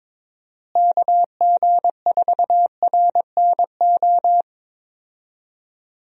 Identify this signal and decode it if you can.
Morse code, heard as a keyed tone, about 22 wpm, 700 Hz. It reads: KG4RNO
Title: USA Call Signs - 6 Characters - Repeated 3x - 22wpm